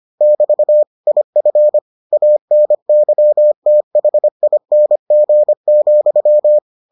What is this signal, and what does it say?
Morse code, heard as a keyed tone, about 25 wpm, 600 Hz. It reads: <BT> IF ANYTHING,